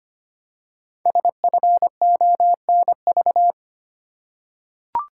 Morse code, heard as a keyed tone, about 25 wpm, 700 Hz. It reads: SFONV
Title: Code Groups - 5 Letters - 25wpm